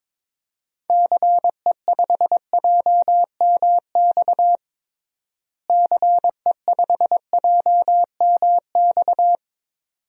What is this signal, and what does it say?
Morse code, heard as a keyed tone, about 22 wpm, 700 Hz. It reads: CE5JMX CE5JMX
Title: DXCC-100 Call Signs - 6 Characters - Repeated 2x - 22wpm